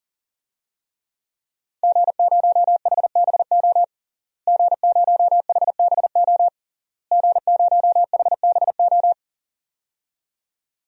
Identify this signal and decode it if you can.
Morse code, heard as a keyed tone, about 37 wpm, 700 Hz. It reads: G0HBO G0HBO G0HBO
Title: DXCC-100 Call Signs - 5 Characters - Repeated 3x - 40wpm